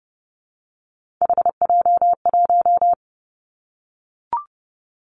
Morse code, heard as a keyed tone, about 30 wpm, 700 Hz. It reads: HJ1
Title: Three Challenging Letter-Number Combinations 30wpm